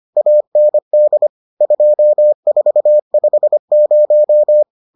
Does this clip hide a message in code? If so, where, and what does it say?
Yes: Morse code from 0.2 to 4.6 s, heard as a keyed tone, about 25 wpm, 600 Hz. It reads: AND 2450